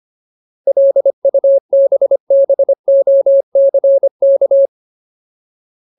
Morse code, heard as a keyed tone, about 25 wpm, 550 Hz. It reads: LUBBOCK
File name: Ultimate Word List / 25WPM Ultimate